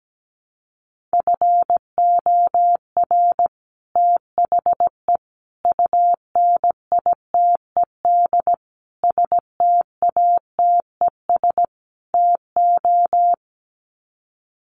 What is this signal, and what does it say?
Morse code, heard as a keyed tone, about 17 wpm, 700 Hz. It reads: FOR THE UNITED STATES TO
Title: Sets of 5 Words Encore 17wpm